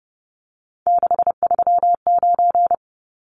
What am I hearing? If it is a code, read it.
Morse code, heard as a keyed tone, about 30 wpm, 700 Hz. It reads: 639
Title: Numbers - 3-Digit 30wpm